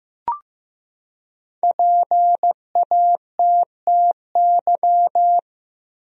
Morse code, heard as a keyed tone, about 15 wpm, 700 Hz. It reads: PATTY
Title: USA Common Names - 15wpm